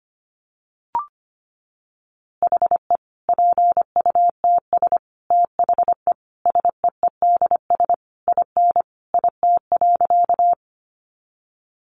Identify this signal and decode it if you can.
Morse code, heard as a keyed tone, about 25 wpm, 700 Hz. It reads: HE PUTS THE SEEDS IN IT.
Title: Easy Phrases 25wpm